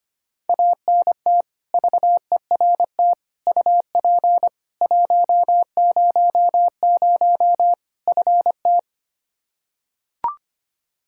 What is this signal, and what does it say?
Morse code, heard as a keyed tone, about 25 wpm, 700 Hz. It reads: ANT VERT UP 100 FT
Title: Practice QSO - Exchange 2 - Info - 25wpm